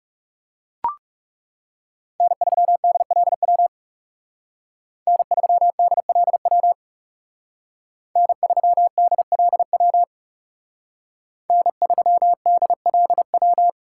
Morse code, heard as a keyed tone, about 36 wpm, 700 Hz. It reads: N3DLW N3DLW N3DLW N3DLW
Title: Speed Racing - USA 1x3 Call Signs 30wpm